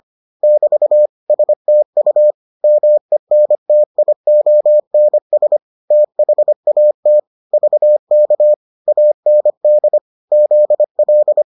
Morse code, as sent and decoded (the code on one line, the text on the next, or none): -...- / ... - ..- / -- . -. - .. --- -. ... / - .... .- - / ...- -.- / .- -. -.. / --.. .-..
<BT> STU MENTIONS THAT VK AND ZL